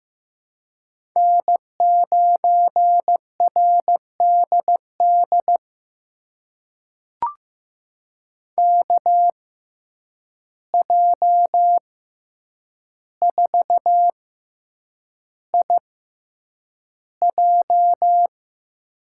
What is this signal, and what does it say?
Morse code, heard as a keyed tone, about 15 wpm, spaced as for 8 wpm, 700 Hz. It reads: N9RDD KJ4IJ
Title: USA Call Signs - 5 Characters - 6x Character Spacing - 15wpm